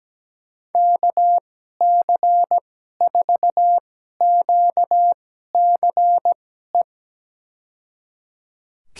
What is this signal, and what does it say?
Morse code, heard as a keyed tone, about 17 wpm, spaced as for 12 wpm, 700 Hz. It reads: KC4QCE
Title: USA Call Signs - 2x Character Spacing - 17wpm